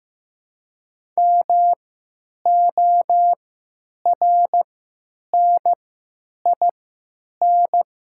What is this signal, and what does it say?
Morse code, heard as a keyed tone, about 15 wpm, spaced as for 8 wpm, 700 Hz. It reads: MORNIN
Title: 3x Character Spacing - Top 500 Words - 15wpm